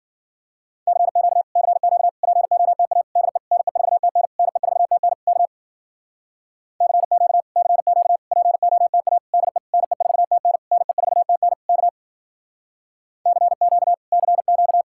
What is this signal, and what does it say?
Morse code, heard as a keyed tone, about 51 wpm, 700 Hz. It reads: CQ CQ POTA DE NE4TN NE4TN K CQ CQ POTA DE NE4TN NE4TN K CQ CQ